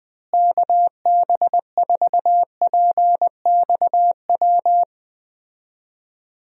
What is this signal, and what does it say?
Morse code, heard as a keyed tone, about 20 wpm, 700 Hz. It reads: KB4PXW
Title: USA Call Signs - 6 Characters - Repeated 2x - 20wpm